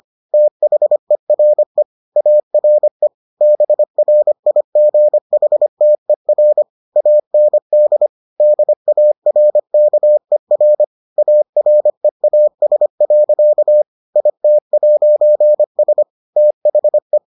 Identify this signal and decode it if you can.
Morse code, heard as a keyed tone, about 25 wpm, 600 Hz. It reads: THERE ARE BRIGHTER AND DARKER AREAS. IT'S THE